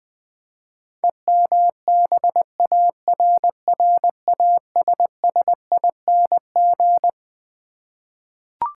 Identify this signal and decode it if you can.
Morse code, heard as a keyed tone, about 21 wpm, 700 Hz. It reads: EMBARRASSING
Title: Twelve Letter Words - 20wpm